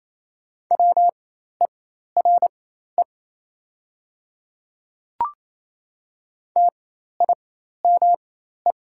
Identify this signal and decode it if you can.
Morse code, heard as a keyed tone, about 28 wpm, spaced as for 13 wpm, 700 Hz. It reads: WERE TIME
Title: 4x Character Spacing - Top 100 Words - 28wpm